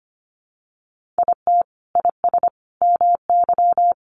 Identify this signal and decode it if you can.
Morse code, heard as a keyed tone, about 24 wpm, 700 Hz. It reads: IT IS MY
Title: Sentences from Top 1000 Words 25wpm